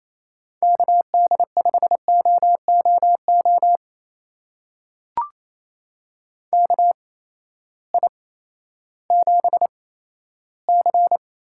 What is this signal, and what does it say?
Morse code, heard as a keyed tone, about 28 wpm, spaced as for 14 wpm, 700 Hz. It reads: KD5OOO KI7C
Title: USA Call Signs - 6 Characters - 8x Character Spacing - 28wpm